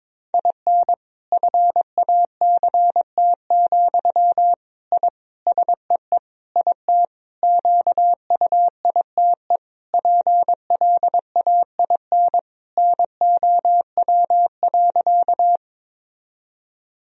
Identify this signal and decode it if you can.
Morse code, heard as a keyed tone, about 23 wpm, 700 Hz. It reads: IN FACT, I SEE IT QUITE PLAIN NOW.
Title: Sentences from Top 1000 Words - Review Part 3 - 22wpm